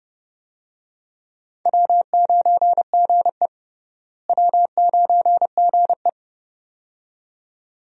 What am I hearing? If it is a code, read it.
Morse code, heard as a keyed tone, about 31 wpm, 700 Hz. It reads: W9GE W9GE
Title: USA Call Signs - 4 Characters - Repeated 2x - 30wpm